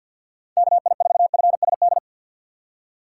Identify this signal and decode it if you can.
Morse code, heard as a keyed tone, about 48 wpm, 700 Hz. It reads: KI4FSD
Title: USA Call Signs - 6 Characters - 6x Character Spacing - 50wpm